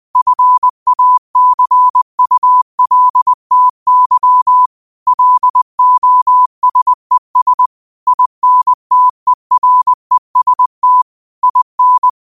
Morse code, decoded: FACULTY LOSES INTEREST IN

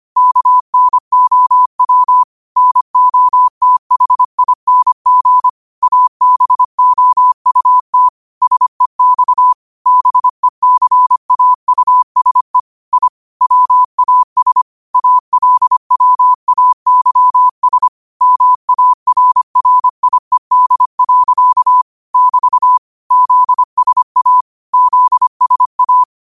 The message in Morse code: KNOW NOTHING ABOUT SEX BECAUSE I WAS ALWAYS MARRIED. <BT> ZSA ZSA

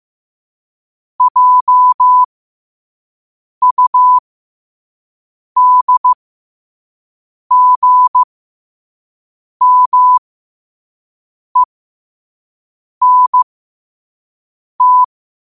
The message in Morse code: JUDGMENT

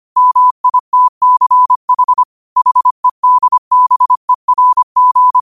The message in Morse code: MITCH HEDBERG

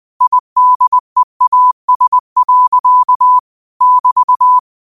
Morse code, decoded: IDEAS. <BT>

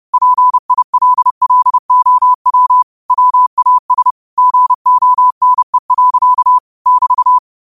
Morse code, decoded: PILLOW WAS GONE. <BT>

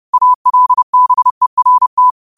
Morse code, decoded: ALBERT

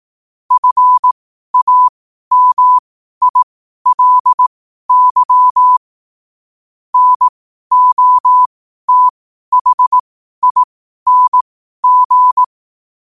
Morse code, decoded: FAMILY NOTHING

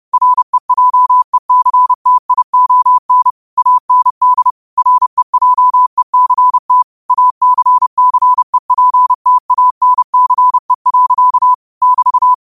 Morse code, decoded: REJECTION AND REJECT ACCEPTANCE. <BT>

